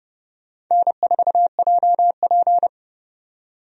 Morse code, heard as a keyed tone, about 29 wpm, 700 Hz. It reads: N4JP
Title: USA Call Signs - 4 Characters - 30wpm